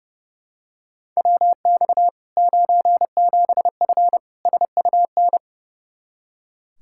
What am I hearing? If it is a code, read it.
Morse code, heard as a keyed tone, about 30 wpm, 700 Hz. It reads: WX 97F SUN
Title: Practice QSO - Exchange 2 - Info - 30wpm